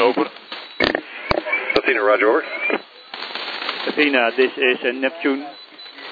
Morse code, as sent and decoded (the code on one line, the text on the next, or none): none